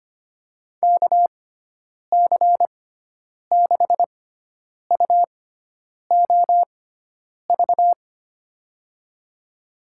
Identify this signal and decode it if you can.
Morse code, heard as a keyed tone, about 25 wpm, spaced as for 9 wpm, 700 Hz. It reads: KC6UOV
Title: USA Call Signs - 6 Characters - 6x Character Spacing - 25wpm